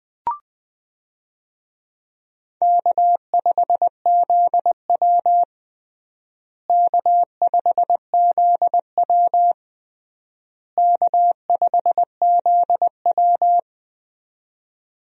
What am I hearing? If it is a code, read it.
Morse code, heard as a keyed tone, about 21 wpm, 700 Hz. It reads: K5ZW K5ZW K5ZW
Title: USA Call Signs - 4 Characters - Repeated 3x - 20wpm